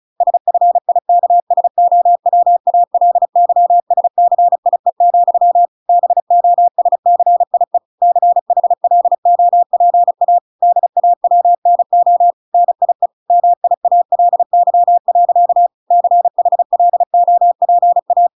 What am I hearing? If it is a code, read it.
Morse code, heard as a keyed tone, about 34 wpm, 700 Hz. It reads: SFIKSOWALYSCIE, BOSCIE CHLOPA DAWNO NIE MIALY. CHLOPA